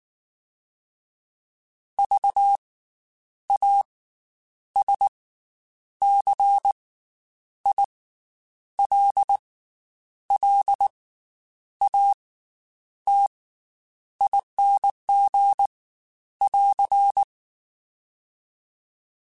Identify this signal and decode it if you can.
Morse code, heard as a keyed tone, about 19 wpm, 800 Hz. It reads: V A S C I L L A T ING <AR>
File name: Special INGs / Special INGs-mp3s / Slow ing @18